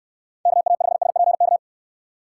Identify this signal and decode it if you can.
Morse code, heard as a keyed tone, about 69 wpm, 700 Hz. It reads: KA5SPK